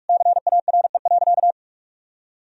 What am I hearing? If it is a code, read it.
Morse code, heard as a keyed tone, about 46 wpm, 700 Hz. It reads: CARE.
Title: Sentences from Top 1000 Words - Review Part 2 - 45wpm